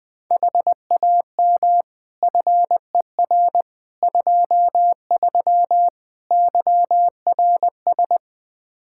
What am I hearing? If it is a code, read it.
Morse code, heard as a keyed tone, about 20 wpm, 700 Hz. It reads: HAM FER 23 YRS
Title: Practice QSO - Exchange 3 - More Info - 20wpm